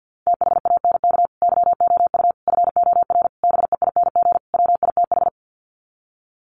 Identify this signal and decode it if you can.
Morse code, heard as a keyed tone, about 48 wpm, 700 Hz. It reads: THANK YOU FOR BEING WITH
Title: Sets of 5 Words 50wpm